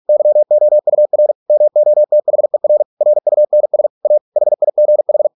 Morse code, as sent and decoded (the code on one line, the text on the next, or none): -.-- --- ..- .-. / -- --- - .... .-. / .-. ..- -. ... / .- / .... .. --. ....
YOUR MOTHER RUNS A HIGH